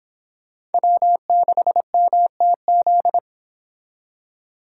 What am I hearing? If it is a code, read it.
Morse code, heard as a keyed tone, about 30 wpm, 700 Hz. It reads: W6MTZ